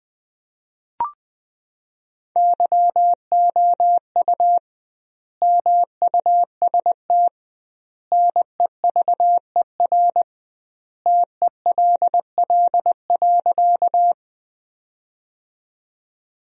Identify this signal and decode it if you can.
Morse code, heard as a keyed tone, about 20 wpm, 700 Hz. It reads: YOU MUST NEVER TELL.
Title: 2x Word-Spacing - Sentences from Top 300 Words - 20wpm